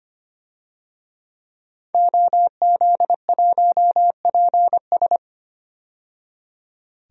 Morse code, heard as a keyed tone, about 25 wpm, 700 Hz. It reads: OZ1PS
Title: DXCC-100 Call Signs - 5 Characters - 25wpm